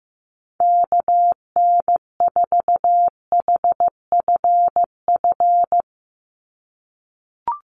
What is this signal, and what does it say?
Morse code, heard as a keyed tone, about 15 wpm, 700 Hz. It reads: KN4HFF
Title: USA Call Signs 15wpm